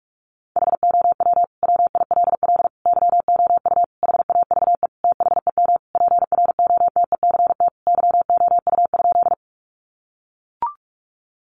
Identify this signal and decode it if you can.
Morse code, heard as a keyed tone, about 44 wpm, 700 Hz. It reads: HOW WILL YOU HAVE THEM PROTECT YOU?
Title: Sentences from Top 900 Words 45wpm